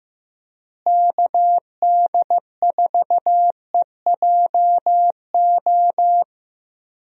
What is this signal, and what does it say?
Morse code, heard as a keyed tone, about 15 wpm, 700 Hz. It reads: KD4EJO
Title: USA Call Signs - 6 Characters - Repeated 3x - 15wpm